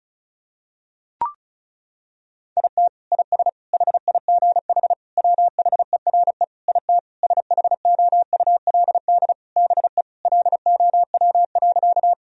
Morse code, heard as a keyed tone, about 34 wpm, 700 Hz. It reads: IT IS HIGH WHERE IT SHOULD BE LOW.
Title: Sentences from Top 1000 Words - Review Part 1 - 35wpm